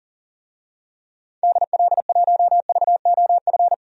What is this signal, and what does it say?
Morse code, heard as a keyed tone, about 40 wpm, 700 Hz. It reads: DL1VOF
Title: DXCC-100 Call Signs - 6 Characters - 40wpm